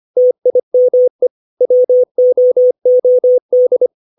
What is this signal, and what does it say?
Morse code, heard as a keyed tone, about 25 wpm, 500 Hz. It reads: TIME WOOD
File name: 632 words / 632 Words 25WPM